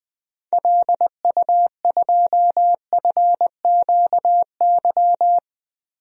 Morse code, heard as a keyed tone, about 20 wpm, 700 Hz. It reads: LU2FQY
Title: DXCC-100 Call Signs - 6 Characters - Repeated 2x - 20wpm